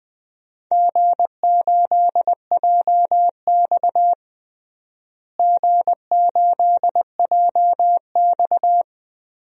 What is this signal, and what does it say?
Morse code, heard as a keyed tone, about 21 wpm, 700 Hz. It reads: G8JX G8JX
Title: DXCC-100 Call Signs - 4 Characters - Repeated 2x - 20wpm